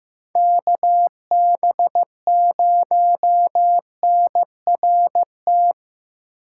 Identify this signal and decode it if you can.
Morse code, heard as a keyed tone, about 15 wpm, 700 Hz. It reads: KB0NRT
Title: USA Call Signs - 6 Characters - 4x Character Spacing - 15wpm